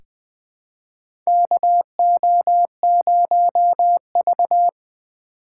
Morse code, heard as a keyed tone, about 20 wpm, 700 Hz. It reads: KO0V
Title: USA Call Signs - 3x Character Spacing - 20wpm